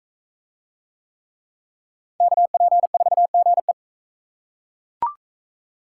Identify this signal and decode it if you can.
Morse code, heard as a keyed tone, about 40 wpm, 700 Hz. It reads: KP4GE